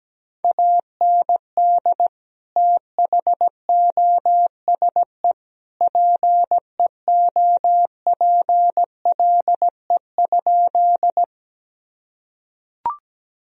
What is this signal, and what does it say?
Morse code, heard as a keyed tone, about 17 wpm, 700 Hz. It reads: AND THOSE PEOPLE?
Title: Sentences from Top 1000 Words - Review Part 1 - 17wpm